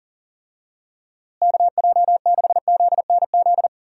Morse code, heard as a keyed tone, about 40 wpm, 700 Hz. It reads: KJ6ZNZ
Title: USA Call Signs - 6 Characters - Repeated 3x - 40wpm